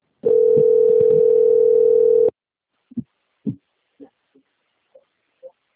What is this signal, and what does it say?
Ringback tone (North America), 440 + 480 Hz.
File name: out- - - - - .6906